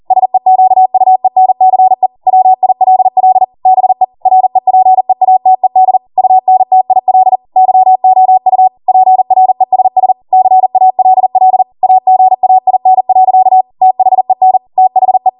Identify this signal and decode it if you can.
Morse code, heard as a keyed tone, about 40 wpm, 750 Hz. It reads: SEQUENCE WILL BE REPEATED UNTIL YOU PRESS CALL AGAIN. THEN THE